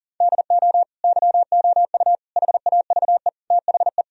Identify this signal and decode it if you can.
Morse code, heard as a keyed tone, about 37 wpm, 700 Hz. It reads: DO YOU HAVE THE
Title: Sentences from Top 1000 Words - Review Part 4 - 40wpm